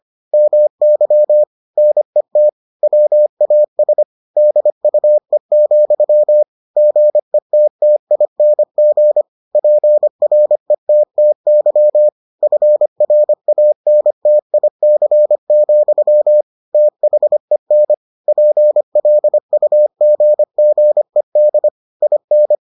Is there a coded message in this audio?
Yes: Morse code from 0.3 to 22.5 s, heard as a keyed tone, about 25 wpm, 600 Hz. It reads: MY NET WAS DUE, GETTING PRETTY FRANTIC, THEN PLUGGED IN